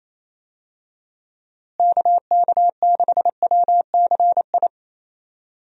Morse code, heard as a keyed tone, about 28 wpm, 700 Hz. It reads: KK6WCI
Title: USA Call Signs - 6 Characters - Repeated 2x - 28wpm